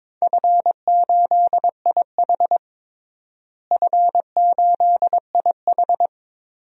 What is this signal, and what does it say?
Morse code, heard as a keyed tone, about 22 wpm, 700 Hz. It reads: F8IH F8IH
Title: DXCC-100 Call Signs - 4 Characters - Repeated 3x - 22wpm